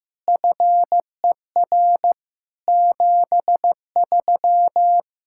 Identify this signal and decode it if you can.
Morse code, heard as a keyed tone, about 15 wpm, 700 Hz. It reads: FER 73
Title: Practice QSO - Exchange 3 - More Info - 15wpm